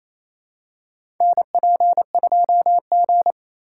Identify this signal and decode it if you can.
Morse code, heard as a keyed tone, about 28 wpm, 700 Hz. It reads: NP2G
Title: USA Call Signs - 4 Characters - 4x Character Spacing - 28wpm